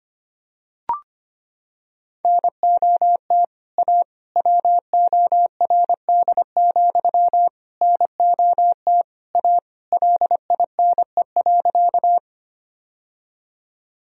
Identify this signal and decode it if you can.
Morse code, heard as a keyed tone, about 25 wpm, 700 Hz. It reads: NOT A WORD, NOT A LINE.
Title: Sentences from Top 1000 Words - Review Part 1 - 25wpm